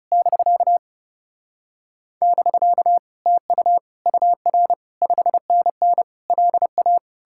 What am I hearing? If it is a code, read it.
Morse code, heard as a keyed tone, about 37 wpm, 700 Hz. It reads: <BK> <BK> TU UR 5NN LA
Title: Speed-Racing Common QSO Phrases Encore - 30wpm